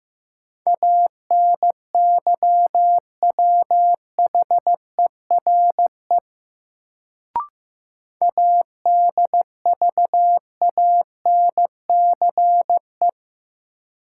Morse code, heard as a keyed tone, about 15 wpm, 700 Hz. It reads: ANYWHERE ADVANCE
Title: Sentences from Top 1400 Words - New Word Drill Alt - 15wpm